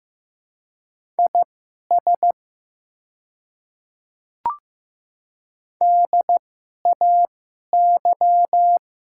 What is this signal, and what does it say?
Morse code, heard as a keyed tone, about 15 wpm, spaced as for 11 wpm, 700 Hz. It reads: IS DAY
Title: 2x Character Spacing - Top 200 Words - 15wpm